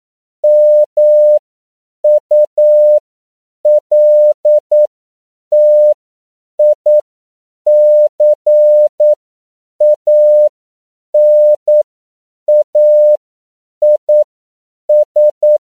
Morse code, heard as a keyed tone, about 9 wpm, spaced as for 7 wpm, 600 Hz. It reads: MULTICANAIS
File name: CW / 10PPM